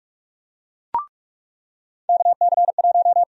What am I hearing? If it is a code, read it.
Morse code, heard as a keyed tone, about 45 wpm, 700 Hz. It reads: KC1
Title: USA Call Sign Prefixes - 45wpm